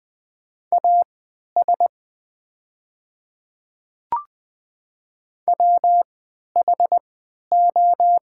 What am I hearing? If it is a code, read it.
Morse code, heard as a keyed tone, about 20 wpm, spaced as for 11 wpm, 700 Hz. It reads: AS WHO
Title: 3x Character Spacing - Top 200 Words - 20wpm